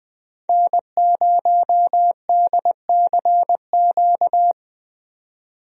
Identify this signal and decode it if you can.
Morse code, heard as a keyed tone, about 20 wpm, 700 Hz. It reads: N0DCQ
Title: USA Call Signs - 5 Characters - 20wpm